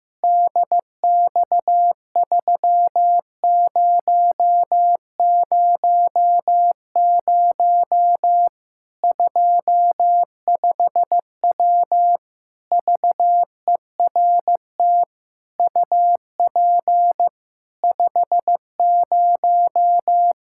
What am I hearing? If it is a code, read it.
Morse code, heard as a keyed tone, about 15 wpm, 700 Hz. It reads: DX3000 25W VERT UP 50
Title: Practice QSO - Exchange 2 - Info - 15wpm